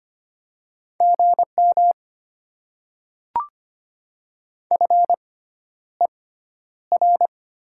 Morse code, heard as a keyed tone, about 25 wpm, spaced as for 11 wpm, 700 Hz. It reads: GM FER
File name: Top 20 QSO Elements - 6x Character Spacing - 25wpm